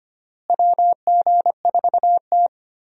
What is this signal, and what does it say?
Morse code, heard as a keyed tone, about 25 wpm, 700 Hz. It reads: WG4T
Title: USA Call Signs - 4 Characters - Repeated 2x - 25wpm